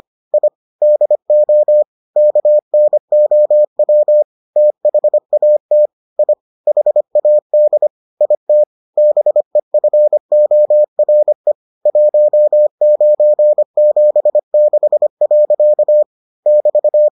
Morse code, heard as a keyed tone, about 25 wpm, 600 Hz. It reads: I DO KNOW THAT I HAD IT BEFORE 1976. <BT>